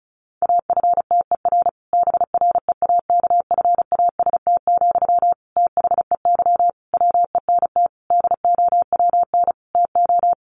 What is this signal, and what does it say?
Morse code, heard as a keyed tone, about 34 wpm, 700 Hz. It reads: AFTER BREAKFAST, THEY WENT DOWN TO
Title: Sentences from Top 1500 Words 35wpm